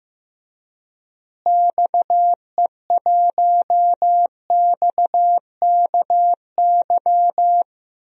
Morse code, heard as a keyed tone, about 15 wpm, 700 Hz. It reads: XE1XKY
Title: DXCC-100 Call Signs - 6 Characters - Repeated 3x - 15wpm